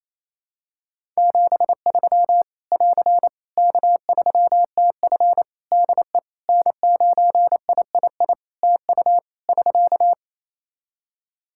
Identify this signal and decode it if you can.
Morse code, heard as a keyed tone, about 28 wpm, 700 Hz. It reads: 73 <AR> K3TF DE N9III TU <SK>
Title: Practice QSO - Exchange 4 - Ending - 28wpm